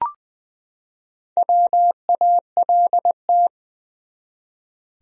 Morse code, heard as a keyed tone, about 20 wpm, 700 Hz. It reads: WALT
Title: USA Common Names - Max 4 Letters - 20wpm